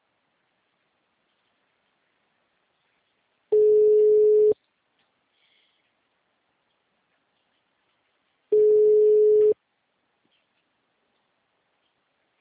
Ringback tone (Europe, 425 Hz).
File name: out- -2002- - - .22